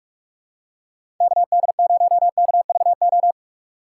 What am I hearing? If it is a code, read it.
Morse code, heard as a keyed tone, about 43 wpm, 700 Hz. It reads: KD0KVO
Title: USA Call Signs - 6 Characters - 6x Character Spacing - 45wpm